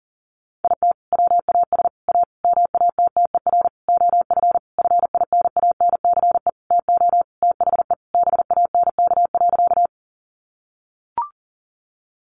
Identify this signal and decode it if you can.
Morse code, heard as a keyed tone, about 40 wpm, 700 Hz. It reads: IT WAS A MATTER OF FINANCE TO THE BANK.
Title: Sentences from Top 1500 Words 40wpm